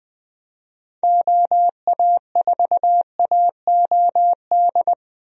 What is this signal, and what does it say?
Morse code, heard as a keyed tone, about 20 wpm, 700 Hz. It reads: OA4AOD
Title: USA Call Signs - 6 Characters - Repeated 2x - 20wpm